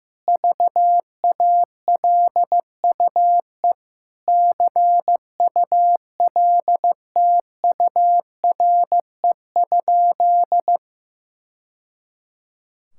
Morse code, heard as a keyed tone, about 15 wpm, 700 Hz. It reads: VALUE CULTURE?
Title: Sentences from Top 1000 Words - Review Part 6 - 15wpm